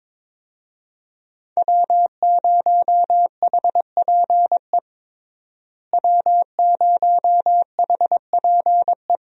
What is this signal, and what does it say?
Morse code, heard as a keyed tone, about 22 wpm, 700 Hz. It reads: W0HPE W0HPE
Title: USA Call Signs - 5 Characters - Repeated 3x - 22wpm